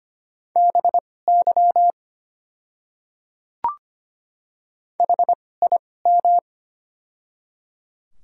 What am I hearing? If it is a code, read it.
Morse code, heard as a keyed tone, about 25 wpm, spaced as for 18 wpm, 700 Hz. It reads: BY HIM
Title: 2x Character Spacing - Top 100 Words - 25wpm